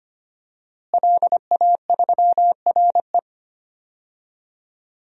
Morse code, heard as a keyed tone, about 25 wpm, 700 Hz. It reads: LA3RE
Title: DXCC-100 Call Signs - 5 Characters - Repeated 2x - 25wpm